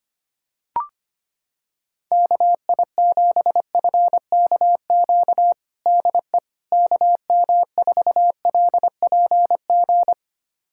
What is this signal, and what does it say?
Morse code, heard as a keyed tone, about 25 wpm, 700 Hz. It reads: KI7FKQ DE KM4LPG
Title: USA Callsign DE Callsign - 6 Characters - 25wpm